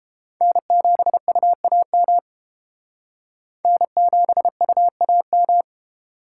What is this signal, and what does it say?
Morse code, heard as a keyed tone, about 30 wpm, 700 Hz. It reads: N7UAM N7UAM